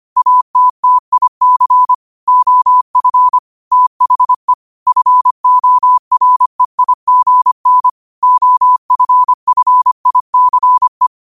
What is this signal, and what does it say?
Morse code, heard as a keyed tone, about 25 wpm, 1000 Hz. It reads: ATTIC OF THE FOREIGN OFFICE
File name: 25wpm